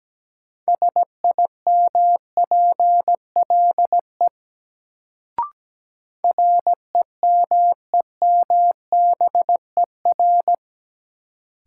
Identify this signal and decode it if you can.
Morse code, heard as a keyed tone, about 17 wpm, 700 Hz. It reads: SIMPLE REMEMBER
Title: Sentences from Top 400 Words - New Word Drill - 17wpm